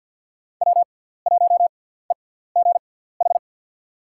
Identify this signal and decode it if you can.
Morse code, heard as a keyed tone, about 46 wpm, spaced as for 17 wpm, 700 Hz. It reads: W1EGH
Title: USA Call Signs - 5 Characters - 6x Character Spacing - 50wpm